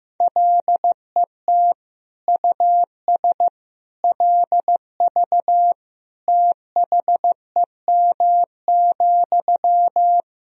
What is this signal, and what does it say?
Morse code, heard as a keyed tone, about 16 wpm, 700 Hz. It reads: LET US LV THEM,
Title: Sentences from Top 400 Words and Abbreviations 15wpm